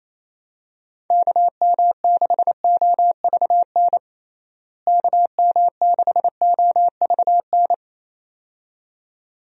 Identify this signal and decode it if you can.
Morse code, heard as a keyed tone, about 28 wpm, 700 Hz. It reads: KM6OVN KM6OVN
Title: USA Call Signs - 6 Characters - Repeated 2x - 28wpm